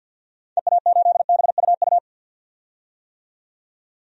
Morse code, heard as a keyed tone, about 47 wpm, 700 Hz. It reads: EA8BUU
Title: DXCC-100 Call Signs - 6 Characters - 50wpm